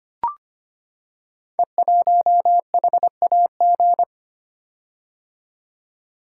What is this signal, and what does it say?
Morse code, heard as a keyed tone, about 25 wpm, 700 Hz. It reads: E1HAG
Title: Code Groups - 5 Letter:Numbers - 25wpm